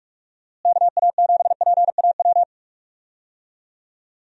Morse code, heard as a keyed tone, about 46 wpm, 700 Hz. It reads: KA7PAW
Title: USA Call Signs - 6 Characters - Repeated 3x - 45wpm